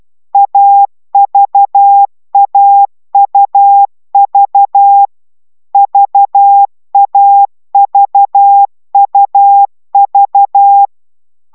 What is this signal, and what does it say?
Morse code, heard as a keyed tone, about 12 wpm, 800 Hz. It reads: AVAUV VAVUV